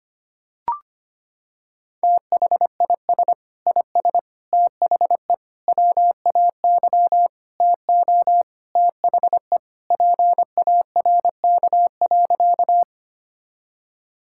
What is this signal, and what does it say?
Morse code, heard as a keyed tone, about 25 wpm, 700 Hz. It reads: THIS IS THE WAY TO THE PARK.
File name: practice / 25wpm / Sentences with 4-Word Phrases - 25wpm